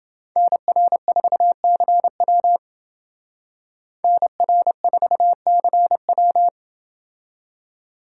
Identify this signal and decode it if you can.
Morse code, heard as a keyed tone, about 28 wpm, 700 Hz. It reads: NR4CW NR4CW